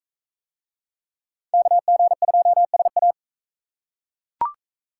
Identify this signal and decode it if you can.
Morse code, heard as a keyed tone, about 41 wpm, 700 Hz. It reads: KG2SA